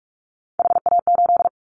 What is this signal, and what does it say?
Morse code, heard as a keyed tone, about 46 wpm, 700 Hz. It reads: HA8
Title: Callsign Prefixes for Top 25 DXCC - 45wpm